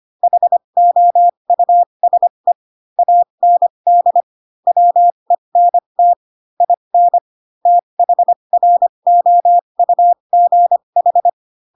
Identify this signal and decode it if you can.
Morse code, heard as a keyed tone, about 25 wpm, spaced as for 22 wpm, 700 Hz. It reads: HOUSE AND WENT IN THROUGH
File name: Pinocchio / Pinocchio ch7 18 25 ik1hge